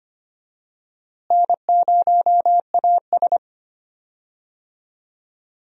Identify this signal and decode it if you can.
Morse code, heard as a keyed tone, about 25 wpm, 700 Hz. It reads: N0AS
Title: USA Call Signs - 4 Characters - Repeated 2x - 25wpm